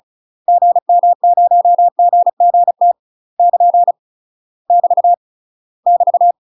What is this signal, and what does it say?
Morse code, heard as a keyed tone, about 34 wpm, 700 Hz. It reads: GM0GGT <KN> <BT> <BT>
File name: UK MASTER QSOs TRAINING / 35WPM / UK MASTER QSOs 0043